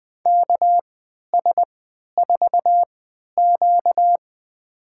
Morse code, heard as a keyed tone, about 21 wpm, spaced as for 12 wpm, 700 Hz. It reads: KS4Q
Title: USA Call Signs - 4 Characters - 3x Character Spacing - 20wpm